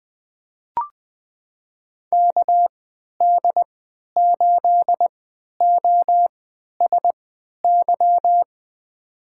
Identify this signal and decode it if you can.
Morse code, heard as a keyed tone, about 21 wpm, spaced as for 12 wpm, 700 Hz. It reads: KD8OSY
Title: USA Call Signs - 6 Characters - 3x Character Spacing - 20wpm